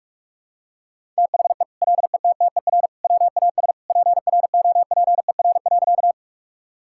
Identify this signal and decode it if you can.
Morse code, heard as a keyed tone, about 46 wpm, 700 Hz. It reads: THE LETTER WAS PROPER.
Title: Sentences from Top 1000 Words - Review Part 5 - 45wpm